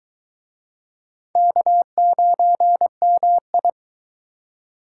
Morse code, heard as a keyed tone, about 23 wpm, 700 Hz. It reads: K9MI